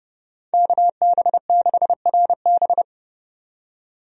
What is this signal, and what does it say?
Morse code, heard as a keyed tone, about 30 wpm, 700 Hz. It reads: KB6RB
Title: USA Call Signs - 5 Characters - 30wpm